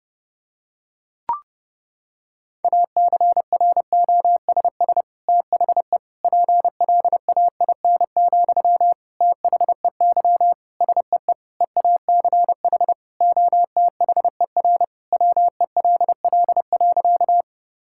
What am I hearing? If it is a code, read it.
Morse code, heard as a keyed tone, about 29 wpm, 700 Hz. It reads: ACROSS THE PLAIN, THEY SEE EACH OTHER WELL.
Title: Sentences from Top 1000 Words - Review Part 5 - 30wpm